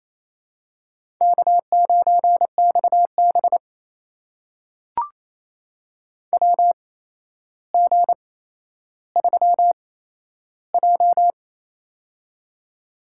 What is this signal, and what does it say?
Morse code, heard as a keyed tone, about 27 wpm, spaced as for 12 wpm, 700 Hz. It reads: K9XB WG3J
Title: USA Call Signs - 4 Characters - 8x Character Spacing - 28wpm